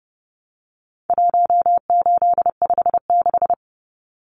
Morse code, heard as a keyed tone, about 29 wpm, 700 Hz. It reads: 1856
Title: Numbers - 4-Digit 30wpm